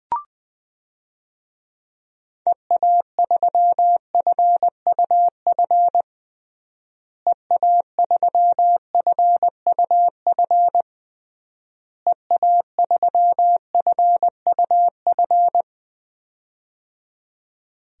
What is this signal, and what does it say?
Morse code, heard as a keyed tone, about 20 wpm, 700 Hz. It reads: EA3FUF EA3FUF EA3FUF
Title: DXCC-100 Call Signs - 6 Characters - Repeated 3x - 20wpm